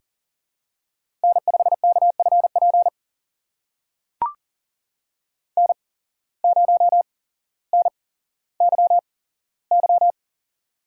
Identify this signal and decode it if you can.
Morse code, heard as a keyed tone, about 41 wpm, spaced as for 17 wpm, 700 Hz. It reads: N5KFP N0NYY
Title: USA Call Signs - 5 Characters - 8x Character Spacing - 40wpm